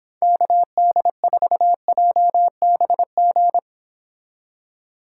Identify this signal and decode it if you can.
Morse code, heard as a keyed tone, about 26 wpm, 700 Hz. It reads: KD4JBG